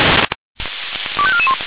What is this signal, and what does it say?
Five-tone selective call (ZVEI), heard as quick runs of five tones, tones of about 70 ms. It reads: 25441